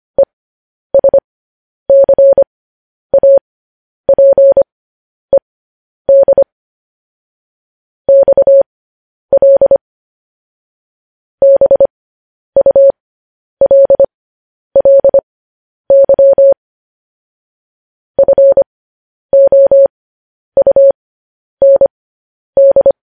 Morse code, heard as a keyed tone, about 25 wpm, spaced as for 10 wpm, 550 Hz. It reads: ESCAPED XL BULLY FOUND
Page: news headlines in morse code at 10 wpm